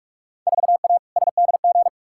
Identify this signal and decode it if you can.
Morse code, heard as a keyed tone, about 41 wpm, 700 Hz. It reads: 4A SDG
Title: Field Day Exchanges - 45wpm